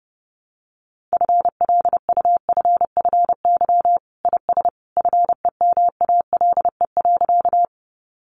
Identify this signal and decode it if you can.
Morse code, heard as a keyed tone, about 30 wpm, 700 Hz. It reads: FLUFFY IS FEMALE.
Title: Easy Phrases 30wpm